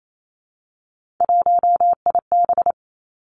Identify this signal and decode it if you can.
Morse code, heard as a keyed tone, about 28 wpm, 700 Hz. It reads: 1IB
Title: Three Challenging Letter-Number Combinations 28wpm